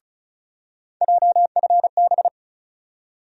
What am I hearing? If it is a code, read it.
Morse code, heard as a keyed tone, about 36 wpm, 700 Hz. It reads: JFB
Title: USA Call Sign Postfixes - 3 Letters 35wpm